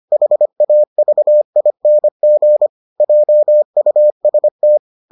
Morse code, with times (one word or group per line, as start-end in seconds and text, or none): 0.1-2.7 HAVING
3.0-4.8 JUST